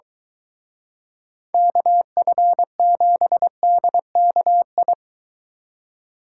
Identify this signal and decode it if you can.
Morse code, heard as a keyed tone, about 23 wpm, 700 Hz. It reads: KF7DKI